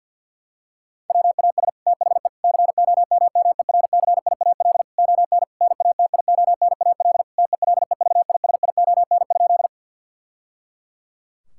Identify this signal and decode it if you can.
Morse code, heard as a keyed tone, about 50 wpm, 700 Hz. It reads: WAS THE COMMERCIAL ON NATIONAL TELEVISION?
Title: Sentences from Top 1000 Words - Review Part 5 - 50wpm